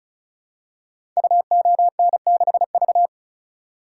Morse code, heard as a keyed tone, about 36 wpm, 700 Hz. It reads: UON6V
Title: Code Groups - 5 Letter:Numbers - 35wpm